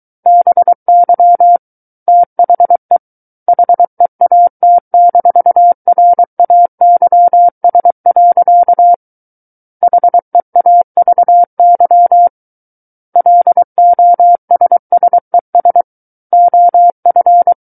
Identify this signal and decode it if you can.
Morse code, heard as a keyed tone, about 23 wpm, 700 Hz. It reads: BY THE HEAT-RAYS. HEAVY LOSSES OF
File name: Book One / 14-Chapter 14 - 23 WPM